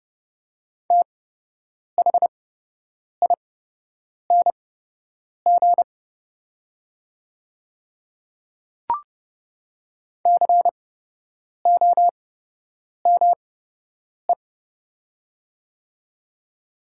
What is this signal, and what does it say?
Morse code, heard as a keyed tone, about 30 wpm, spaced as for 8 wpm, 700 Hz. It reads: THING COME
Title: 8x Character Spacing - Top 100 Words - 30wpm